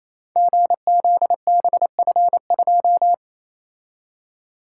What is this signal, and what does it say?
Morse code, heard as a keyed tone, about 29 wpm, 700 Hz. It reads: GZBF2
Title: Code Groups - 5 Letter:Numbers - 28wpm